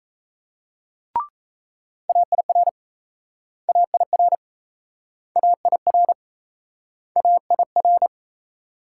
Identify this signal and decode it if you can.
Morse code, heard as a keyed tone, about 38 wpm, 700 Hz. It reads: AIR AIR AIR AIR